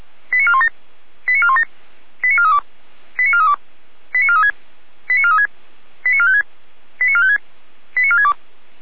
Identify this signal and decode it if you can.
Five-tone selective call (ZVEI), heard as quick runs of five tones, tones of about 70 ms. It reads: 89417 89417 89432 89432 89436 89436 89456 89456 89462